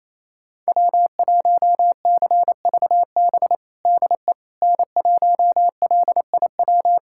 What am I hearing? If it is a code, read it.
Morse code, heard as a keyed tone, about 28 wpm, 700 Hz. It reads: W1CVB DE N1LIW
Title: USA Callsign DE Callsign - 5 Characters - 28wpm